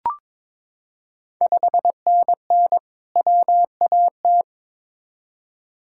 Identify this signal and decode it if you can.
Morse code, heard as a keyed tone, about 22 wpm, 690 Hz. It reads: 5NN WAT
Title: Ontario QSO Party Exchange 22wpm